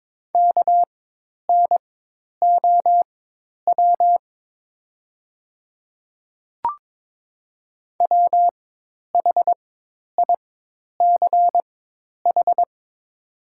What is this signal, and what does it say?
Morse code, heard as a keyed tone, about 22 wpm, spaced as for 10 wpm, 690 Hz. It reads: KNOW WHICH
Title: 4x Character Spacing - Top 200 Words - 22wpm